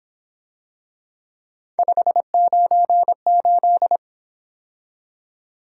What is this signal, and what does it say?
Morse code, heard as a keyed tone, about 26 wpm, 700 Hz. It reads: 598
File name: Speed-Racing - Three Letter-Number Combinations - 22wpm